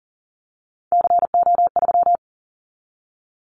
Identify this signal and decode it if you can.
Morse code, heard as a keyed tone, about 40 wpm, 700 Hz. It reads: CO3
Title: Callsign Prefixes for Top 100 DXCC - 40wpm